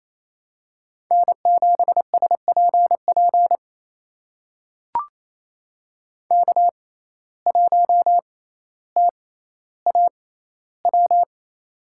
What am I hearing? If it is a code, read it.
Morse code, heard as a keyed tone, about 28 wpm, spaced as for 14 wpm, 700 Hz. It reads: N7SPP K1TAW
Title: USA Call Signs - 5 Characters - 6x Character Spacing - 28wpm